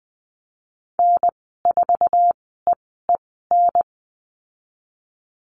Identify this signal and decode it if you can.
Morse code, heard as a keyed tone, about 20 wpm, spaced as for 15 wpm, 700 Hz. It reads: N4EEN
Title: 088 Lesson 15g - Number 5 - Call Signs 20wpm